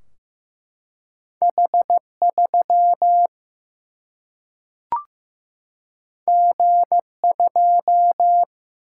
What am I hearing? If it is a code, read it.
Morse code, heard as a keyed tone, about 15 wpm, 700 Hz. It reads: H3 G2
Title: Single-Letter Single-Number 15wpm